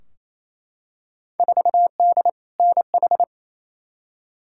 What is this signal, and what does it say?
Morse code, heard as a keyed tone, about 28 wpm, 700 Hz. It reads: 4D NH
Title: Field Day Exchanges Encore - 28wpm